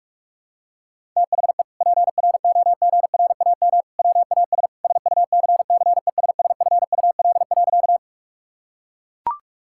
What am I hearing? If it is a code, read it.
Morse code, heard as a keyed tone, about 41 wpm, 700 Hz. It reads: THE PROGRAM WAS SUCCESSFUL.
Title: Sentences from Top 1000 Words - Review Part 5 - 45wpm